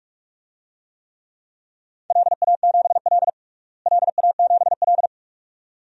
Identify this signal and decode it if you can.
Morse code, heard as a keyed tone, about 46 wpm, 700 Hz. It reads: LA7L LA7L
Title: DXCC-100 Call Signs - 4 Characters - Repeated 2x - 45wpm